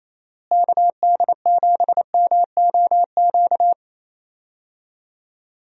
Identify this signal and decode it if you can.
Morse code, heard as a keyed tone, about 28 wpm, 700 Hz. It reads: KD7MOQ
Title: USA Call Signs - 6 Characters - Repeated 3x - 28wpm